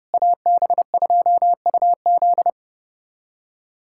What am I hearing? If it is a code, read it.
Morse code, heard as a keyed tone, about 30 wpm, 700 Hz. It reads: AB2UZ
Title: USA Callsign DE Callsign - 5 Characters - 30wpm